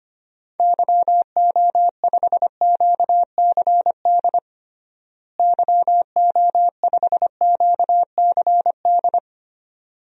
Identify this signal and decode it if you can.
Morse code, heard as a keyed tone, about 25 wpm, 700 Hz. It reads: YO5QCD YO5QCD
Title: DXCC-100 Call Signs - 6 Characters - Repeated 2x - 25wpm